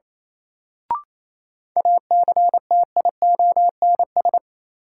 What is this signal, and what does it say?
Morse code, heard as a keyed tone, about 28 wpm, 700 Hz. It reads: ACTIONS
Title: Sentences from Top 1000 Words - Review Part 6 - New Word Drill - 28wpm